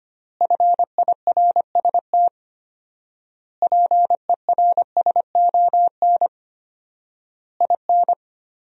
Morse code, heard as a keyed tone, about 25 wpm, 700 Hz. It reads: FIRST PERSON IN
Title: 4x Word-Spacing - Sentences from Top 400 Words - 25wpm